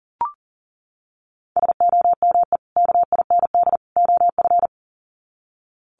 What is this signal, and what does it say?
Morse code, heard as a keyed tone, about 40 wpm, 700 Hz. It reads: SOME KIND OF
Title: Sets of 3 Words 40wpm